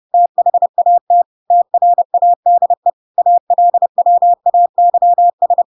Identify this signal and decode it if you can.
Morse code, heard as a keyed tone, about 30 wpm, 700 Hz. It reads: THAT TRADE ALWAYS
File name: Pinocchio / Pinocchio ch4 30 30 ik1hge